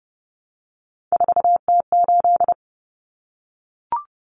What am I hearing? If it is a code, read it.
Morse code, heard as a keyed tone, about 30 wpm, 700 Hz. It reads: 4T8
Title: Common Cut Numbers - 3-Digit 30wpm